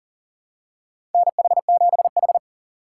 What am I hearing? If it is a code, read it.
Morse code, heard as a keyed tone, about 40 wpm, 700 Hz. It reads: NH7H
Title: USA Call Signs - 4 Characters - 4x Character Spacing - 40wpm